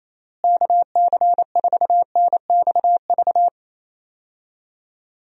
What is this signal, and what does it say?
Morse code, heard as a keyed tone, about 27 wpm, 700 Hz. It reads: KC4NXV
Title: USA Call Signs - 6 Characters - Repeated 3x - 28wpm